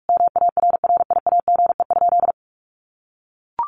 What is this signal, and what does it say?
Morse code, heard as a keyed tone, about 44 wpm, 700 Hz. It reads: MARRIAGE?
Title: Sentences from Top 1500 Words 45wpm